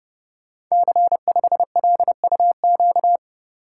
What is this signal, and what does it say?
Morse code, heard as a keyed tone, about 30 wpm, 700 Hz. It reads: C5LUQ
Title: Code Groups - 5 Letter:Numbers - 30wpm